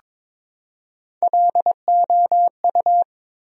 Morse code, heard as a keyed tone, about 22 wpm, 700 Hz. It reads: LOU
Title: USA Common Names - Max 4 Letters - 22wpm